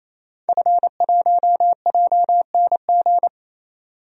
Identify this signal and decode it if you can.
Morse code, heard as a keyed tone, about 28 wpm, 700 Hz. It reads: F1JNG
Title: DXCC-100 Call Signs - 5 Characters - 28wpm